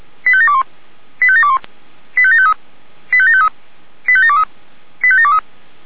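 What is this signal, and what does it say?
Five-tone selective call (ZVEI), heard as quick runs of five tones, tones of about 80 ms. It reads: 85721 85721 85753 85753 85723 85723
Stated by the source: Five-tone ZVEI selective call received by SDR.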